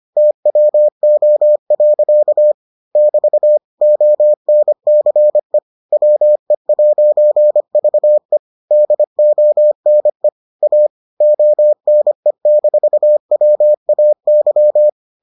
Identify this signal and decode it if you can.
Morse code, heard as a keyed tone, about 25 wpm, 600 Hz. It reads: TWO. <BT> ONCE WE'VE DONE A ONE-WAY